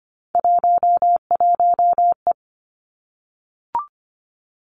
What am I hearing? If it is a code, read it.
Morse code, heard as a keyed tone, about 25 wpm, 700 Hz. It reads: 11E
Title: Three Challenging Letter-Number Combinations 25wpm